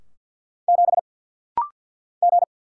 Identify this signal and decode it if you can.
Morse code, heard as a keyed tone, about 50 wpm, 700 Hz. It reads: 7 G
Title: Single Letter-Number Alt - Rapid-Fire 50wpm